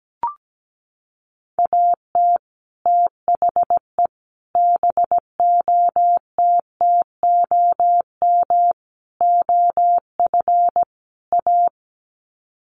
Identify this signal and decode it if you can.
Morse code, heard as a keyed tone, about 17 wpm, 700 Hz. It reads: AT THE BOTTOM OF A
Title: Sets of 5 Words Encore2 17wpm